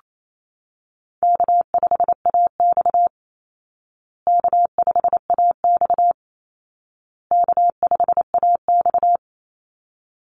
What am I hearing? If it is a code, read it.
Morse code, heard as a keyed tone, about 28 wpm, 700 Hz. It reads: K5AX K5AX K5AX
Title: Easy Call Signs Repeated 3x - 28wpm